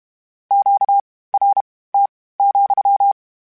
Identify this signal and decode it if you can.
Morse code, heard as a keyed tone, about 32 wpm, spaced as for 18 wpm, 800 Hz. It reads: QRT,
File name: Q Calls / Q Calls-mp3s / Q Calls sppc @30